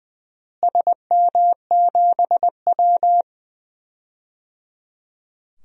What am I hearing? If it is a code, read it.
Morse code, heard as a keyed tone, about 21 wpm, 700 Hz. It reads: SM7W
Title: DXCC-100 Call Signs - 4 Characters - Repeated 3x - 20wpm